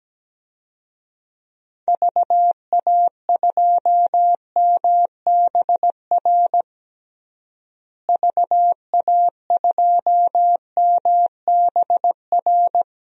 Morse code, heard as a keyed tone, about 17 wpm, 700 Hz. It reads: VA2MBR VA2MBR
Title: DXCC-100 Call Signs - 6 Characters - Repeated 3x - 17wpm